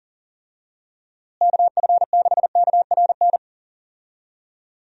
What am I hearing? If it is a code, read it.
Morse code, heard as a keyed tone, about 40 wpm, 700 Hz. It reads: KF6KRN
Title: USA Call Signs - 6 Characters - Repeated 2x - 40wpm